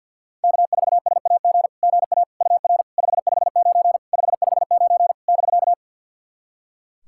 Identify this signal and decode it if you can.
Morse code, heard as a keyed tone, about 50 wpm, 700 Hz. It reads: K4SAG GA UR 559 559 <BK>
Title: SOTA QSO Practice 50wpm